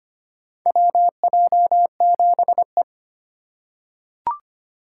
Morse code, heard as a keyed tone, about 25 wpm, 700 Hz. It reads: WJ7E
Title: USA Call Signs - 4 Characters - 2x Character Spacing - 25wpm